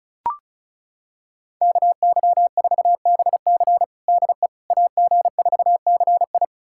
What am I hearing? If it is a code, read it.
Morse code, heard as a keyed tone, about 36 wpm, 700 Hz. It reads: KY4BC DE AG4CI
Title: USA Callsign DE Callsign - 5 Characters - 35wpm